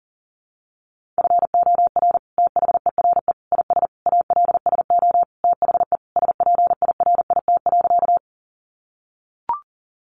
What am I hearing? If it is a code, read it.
Morse code, heard as a keyed tone, about 40 wpm, 700 Hz. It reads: FOR THERE IS ALSO THE SPIRIT.
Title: Sentences from Top 1400 Words 40wpm